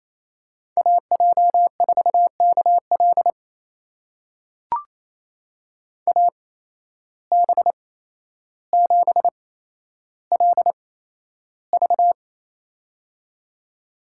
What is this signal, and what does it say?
Morse code, heard as a keyed tone, about 28 wpm, spaced as for 12 wpm, 700 Hz. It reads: AJ4KL AB7LV
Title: USA Call Signs - 5 Characters - 8x Character Spacing - 28wpm